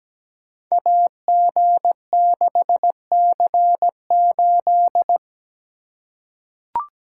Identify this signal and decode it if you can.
Morse code, heard as a keyed tone, about 17 wpm, 700 Hz. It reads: AG6C8
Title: Code Groups - 5 Letter:Numbers - 17wpm